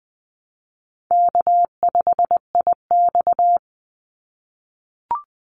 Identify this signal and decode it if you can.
Morse code, heard as a keyed tone, about 20 wpm, 700 Hz. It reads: K5IX
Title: Call Signs - 20wpm